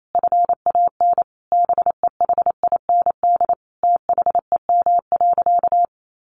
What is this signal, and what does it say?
Morse code, heard as a keyed tone, about 28 wpm, 700 Hz. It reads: FAN BEHIND THEM.
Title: Sentences from Top 1300 Words 28wpm